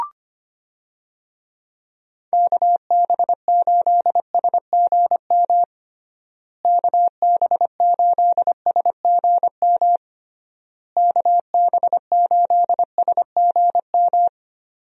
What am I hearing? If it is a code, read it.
Morse code, heard as a keyed tone, about 25 wpm, 700 Hz. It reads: KB8SGM KB8SGM KB8SGM
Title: USA Call Signs - 6 Characters - Repeated 3x - 25wpm